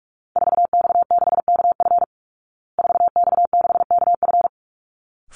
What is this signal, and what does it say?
Morse code, heard as a keyed tone, about 44 wpm, 700 Hz. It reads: 4X6KF 4X6KF
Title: Call Signs Repeated 3x - 45wpm